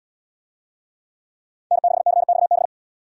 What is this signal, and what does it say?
Morse code, heard as a keyed tone, about 71 wpm, 700 Hz. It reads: N6CXB